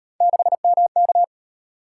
Morse code, heard as a keyed tone, about 32 wpm, 700 Hz. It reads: 6MK